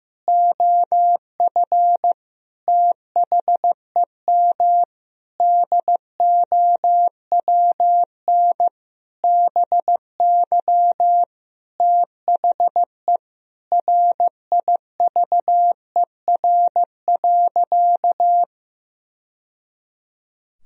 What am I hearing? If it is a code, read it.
Morse code, heard as a keyed tone, about 15 wpm, 700 Hz. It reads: OF THEM DOWN BY THE RIVER.
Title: Sentences from Top 1000 Words - Review Part 1 - 15wpm